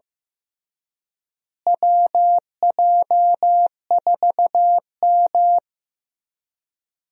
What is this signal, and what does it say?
Morse code, heard as a keyed tone, about 15 wpm, 700 Hz. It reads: WJ4M
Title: USA Call Signs - 4 Characters - Repeated 3x - 15wpm